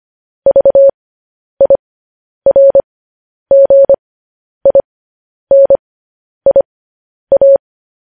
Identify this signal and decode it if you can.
Morse code, heard as a keyed tone, about 25 wpm, spaced as for 10 wpm, 550 Hz. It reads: VIRGINIA
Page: news-headlines-in-morse-code-at-10-wpm